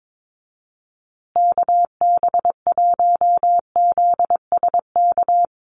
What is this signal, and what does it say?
Morse code, heard as a keyed tone, about 22 wpm, 700 Hz. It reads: KB1ZSK
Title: USA Call Signs Repeated 2x 22wpm